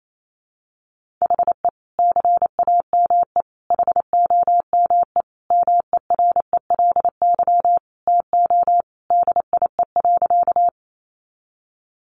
Morse code, heard as a keyed tone, about 28 wpm, 700 Hz. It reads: HE CAME HOME MERELY TO DIE.
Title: Sentences from Top 1400 Words 28wpm